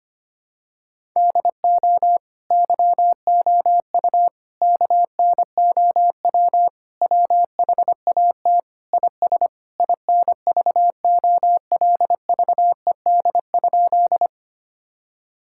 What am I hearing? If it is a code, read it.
Morse code, heard as a keyed tone, about 25 wpm, 700 Hz. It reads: DO YOU KNOW WHAT IS INVOLVED?
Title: Sentences from Top 1000 Words - Review Part 5 - 25wpm